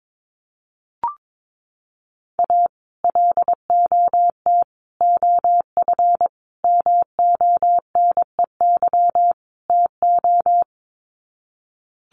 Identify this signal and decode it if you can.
Morse code, heard as a keyed tone, about 22 wpm, 700 Hz. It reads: A LOT OF MONEY TO
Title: Sets of 5 Words Encore2 22wpm